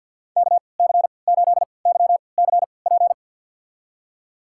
Morse code, heard as a keyed tone, about 51 wpm, spaced as for 28 wpm, 690 Hz. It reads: KC8YCP
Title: USA Call Signs - 6 Characters - 3x Character Spacing - 50wpm